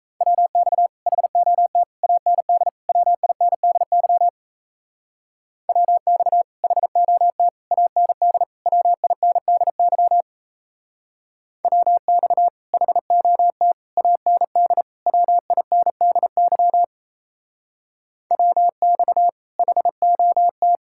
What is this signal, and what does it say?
Morse code, heard as a keyed tone, about 36 wpm, 700 Hz. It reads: WX HOT AND WINDY WX HOT AND WINDY WX HOT AND WINDY WX HOT